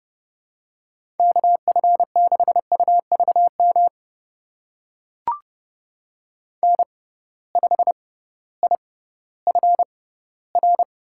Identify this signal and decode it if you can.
Morse code, heard as a keyed tone, about 30 wpm, spaced as for 16 wpm, 700 Hz. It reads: KF6UVM N5IFR
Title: USA Call Signs - 6x Character Spacing - 30wpm